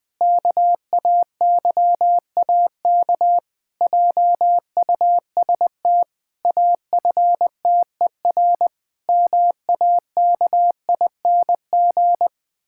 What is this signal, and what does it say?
Morse code, heard as a keyed tone, about 20 wpm, 700 Hz. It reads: KAYAK JUST AFTER MAKING
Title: Pangram Challenge v2 - 20wpm